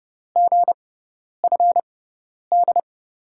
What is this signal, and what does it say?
Morse code, heard as a keyed tone, about 30 wpm, spaced as for 10 wpm, 700 Hz. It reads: GFD
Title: USA Call Signs - 5 Characters - 6x Character Spacing - 30wpm